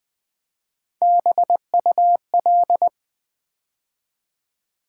Morse code, heard as a keyed tone, about 20 wpm, 700 Hz. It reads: BUL
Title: Build A Word - 3 Letters At A Time - 6-Letter Words - 20wpm